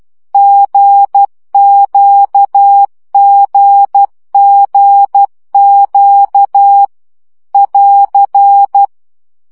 Morse code, heard as a keyed tone, about 12 wpm, 800 Hz. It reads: GQGGQ <AR>
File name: CW gyakorlas g z q 10 baud